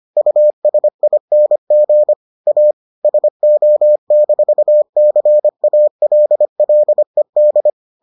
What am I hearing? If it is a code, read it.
Morse code, heard as a keyed tone, about 25 wpm, 600 Hz. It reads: USING A SO-CALLED